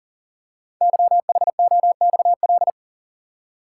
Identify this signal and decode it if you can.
Morse code, heard as a keyed tone, about 38 wpm, 700 Hz. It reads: YHOXL
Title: Code Groups - 5 Letters - 40wpm